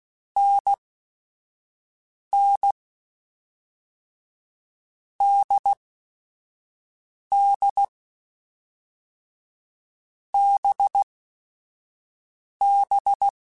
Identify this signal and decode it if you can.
Morse code, heard as a keyed tone, about 16 wpm, spaced as for 4 wpm, 800 Hz. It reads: NNDDBB
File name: Short Runs / Short Runs-mp3s / NDB NDB6 @5